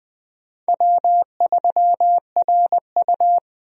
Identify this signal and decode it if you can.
Morse code, heard as a keyed tone, about 20 wpm, 700 Hz. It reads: W3RU
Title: USA Call Signs - 4 Characters - 20wpm